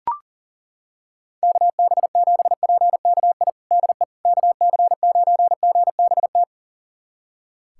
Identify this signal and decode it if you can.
Morse code, heard as a keyed tone, about 40 wpm, 700 Hz. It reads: KB7PKI DE KC9GBT
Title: USA Callsign DE Callsign - 6 Characters - 40wpm